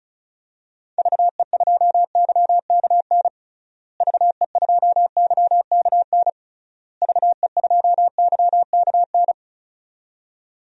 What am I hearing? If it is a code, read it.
Morse code, heard as a keyed tone, about 36 wpm, 700 Hz. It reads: VE2YKN VE2YKN VE2YKN
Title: DXCC-100 Call Signs - 6 Characters - Repeated 3x - 35wpm